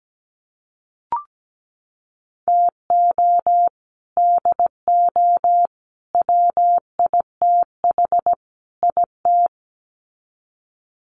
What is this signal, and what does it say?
Morse code, heard as a keyed tone, about 17 wpm, 700 Hz. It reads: TO DO WITH IT
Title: Sets of 4 Words 17wpm